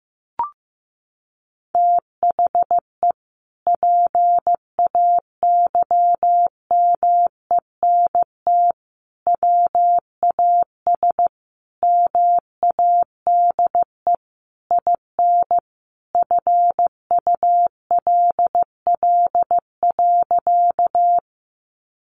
Morse code, heard as a keyed tone, about 15 wpm, 700 Hz. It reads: THE PAYMENT WAS MADE IN FULL.
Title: Sentences from Top 1200 Words 15wpm